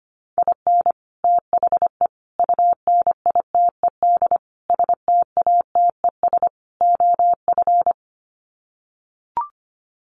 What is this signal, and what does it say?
Morse code, heard as a keyed tone, about 25 wpm, 700 Hz. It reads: IN THE UNITED STATES OF
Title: Sets of 5 Words Encore2 25wpm